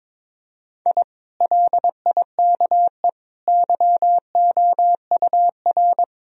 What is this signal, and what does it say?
Morse code, heard as a keyed tone, about 21 wpm, 700 Hz. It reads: I LIKE YOUR
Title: Sentences from Top 1000 Words - Review Part 4 - 22wpm